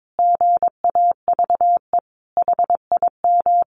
Morse code, heard as a keyed tone, about 22 wpm, 700 Hz. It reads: GAVE HIM
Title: Sentences from Top 1300 Words 22wpm